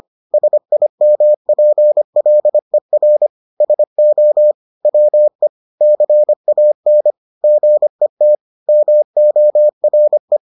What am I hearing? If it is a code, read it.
Morse code, heard as a keyed tone, about 25 wpm, 600 Hz. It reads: SIMPLER SO WE CAN GET MORE